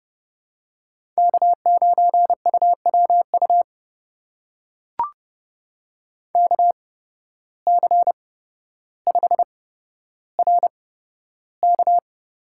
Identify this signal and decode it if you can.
Morse code, heard as a keyed tone, about 30 wpm, spaced as for 13 wpm, 700 Hz. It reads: K9UWU KC5RK
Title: USA Call Signs - 5 Characters - 8x Character Spacing - 30wpm